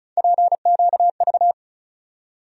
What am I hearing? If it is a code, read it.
Morse code, heard as a keyed tone, about 36 wpm, 700 Hz. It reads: PQV
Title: USA Call Sign Postfixes - 3 Letters 35wpm